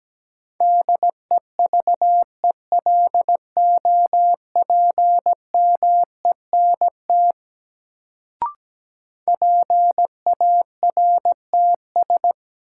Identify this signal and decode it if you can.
Morse code, heard as a keyed tone, about 17 wpm, 700 Hz. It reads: DEVELOPMENT PARTS
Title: Sets of 3 Words - New Word Drill - 17wpm